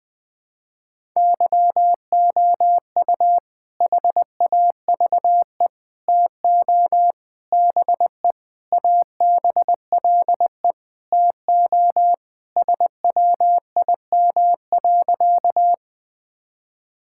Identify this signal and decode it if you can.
Morse code, heard as a keyed tone, about 20 wpm, 700 Hz. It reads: YOU HAVE TO BE ABLE TO SWIM.
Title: Sentences with 5-Word Phrases - 20wpm